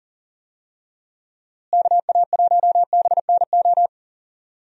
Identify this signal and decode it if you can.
Morse code, heard as a keyed tone, about 40 wpm, 700 Hz. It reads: KA1BNO
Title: USA Call Signs - 6 Characters - Repeated 3x - 40wpm